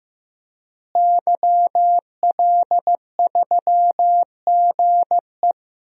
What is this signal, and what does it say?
Morse code, heard as a keyed tone, about 15 wpm, 700 Hz. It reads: YL3GE
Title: DXCC-100 Call Signs - 5 Characters - Repeated 3x - 15wpm